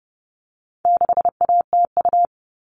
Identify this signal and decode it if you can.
Morse code, heard as a keyed tone, about 30 wpm, 700 Hz. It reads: 6ATU
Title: All Cut Numbers - 4-Digit 30wpm